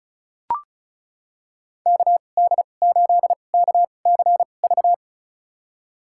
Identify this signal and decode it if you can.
Morse code, heard as a keyed tone, about 34 wpm, spaced as for 25 wpm, 700 Hz. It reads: KD8KCV
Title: USA Call Signs - 2x Character Spacing - 35wpm